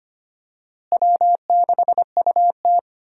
Morse code, heard as a keyed tone, about 25 wpm, 700 Hz. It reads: W6UT
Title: USA Call Signs - 4 Characters - Repeated 2x - 25wpm